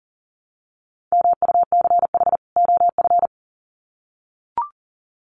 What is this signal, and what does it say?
Morse code, heard as a keyed tone, about 40 wpm, 700 Hz. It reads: MUCH OF
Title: Sets of 2 Words 40wpm